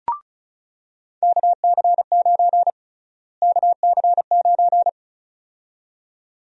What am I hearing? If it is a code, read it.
Morse code, heard as a keyed tone, about 34 wpm, 700 Hz. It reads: KC9 KC9
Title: USA Call Sign Prefixes - Repeated 2x - 35wpm